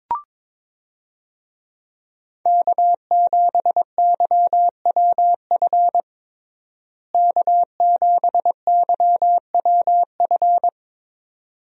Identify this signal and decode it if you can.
Morse code, heard as a keyed tone, about 22 wpm, 700 Hz. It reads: K7YWF K7YWF
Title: USA Call Signs - 5 Characters - Repeated 2x - 22wpm